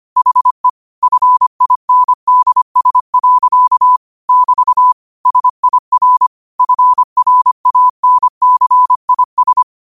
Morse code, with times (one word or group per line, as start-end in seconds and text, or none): 0.2-0.7 HE
1.0-4.0 FINDS.
4.3-4.9 <BT>
5.3-6.3 SIR
6.6-9.6 FRANCIS